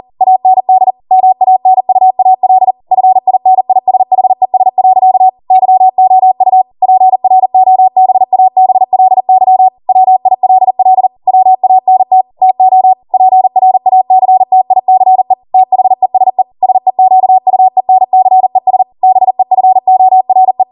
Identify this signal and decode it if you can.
Morse code, heard as a keyed tone, about 40 wpm, 750 Hz. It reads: AND MANUAL FINISHES. YOU PROBABLY WILL WANT TO PRACTICE THESE SEQUENCES BEFORE